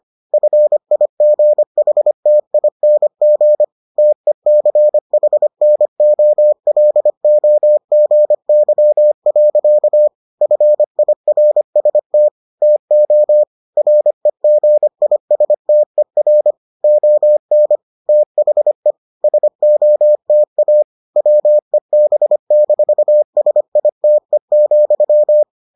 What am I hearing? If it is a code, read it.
Morse code, heard as a keyed tone, about 25 wpm, 600 Hz. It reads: FIGHTING TECHNOLOGY. FIRST TO REGISTER ON THE SOTA WEB-SITE,